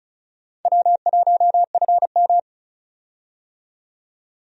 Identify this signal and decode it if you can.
Morse code, heard as a keyed tone, about 34 wpm, 700 Hz. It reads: W1FM
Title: USA Call Signs - 4 Characters - 35wpm